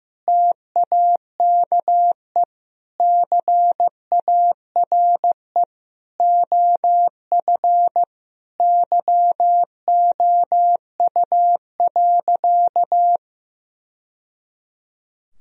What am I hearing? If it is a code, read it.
Morse code, heard as a keyed tone, about 16 wpm, 700 Hz. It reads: TAKE CARE OF YOU.
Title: Sentences from Top 1000 Words - Review Part 1 - 15wpm